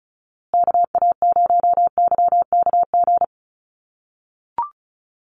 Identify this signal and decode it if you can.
Morse code, heard as a keyed tone, about 34 wpm, 700 Hz. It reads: KA0YKG
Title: USA Call Signs Repeated 3x 35wpm